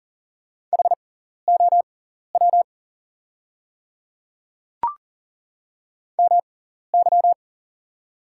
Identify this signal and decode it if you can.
Morse code, heard as a keyed tone, about 40 wpm, spaced as for 14 wpm, 700 Hz. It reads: HOW MY
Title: 6x Character Spacing - Top 100 Words - 40wpm